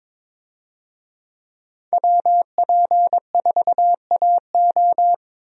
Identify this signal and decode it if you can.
Morse code, heard as a keyed tone, about 23 wpm, 700 Hz. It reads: WP4AO
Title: USA Call Signs - 5 Characters - 22wpm